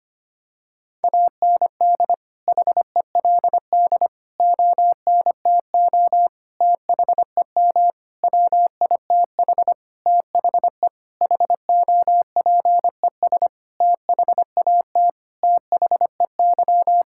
Morse code, heard as a keyed tone, about 25 wpm, 700 Hz. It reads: AND HELD ONTO THEM WITH THE HOPES THAT THEY
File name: practice / 25wpm / Sentences with 9-Word Phrases - 25wpm